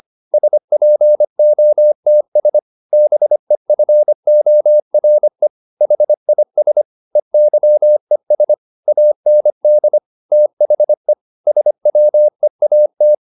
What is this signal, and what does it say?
Morse code, heard as a keyed tone, about 25 wpm, 600 Hz. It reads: SPOTS BEFORE HIS EYES AND THE SWEAT